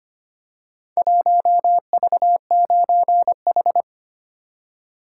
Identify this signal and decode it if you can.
Morse code, heard as a keyed tone, about 25 wpm, 700 Hz. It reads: 1V9H
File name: practice / 25wpm / Four Letter-Number Combinations - 25wpm